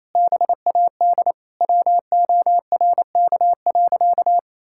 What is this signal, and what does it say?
Morse code, heard as a keyed tone, about 28 wpm, 700 Hz. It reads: BAD WORK.
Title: Sentences from Top 1000 Words - Review Part 6 - 28wpm